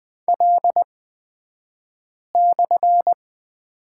Morse code, heard as a keyed tone, about 20 wpm, spaced as for 5 wpm, 700 Hz. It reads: L/
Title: 242 Lesson 39d - Letter z - Two Characters 20wpm